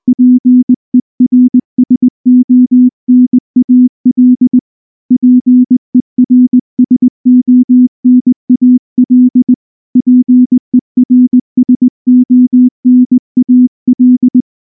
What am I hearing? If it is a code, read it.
Morse code, heard as a keyed tone, about 19 wpm, 250 Hz. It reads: PERSONAL PERSONAL PERSONAL
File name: morse / morse-training-20WPM-7